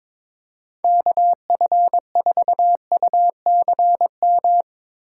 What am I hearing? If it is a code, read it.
Morse code, heard as a keyed tone, about 22 wpm, 700 Hz. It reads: KF4UCM
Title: USA Call Signs - 4x Character Spacing - 22wpm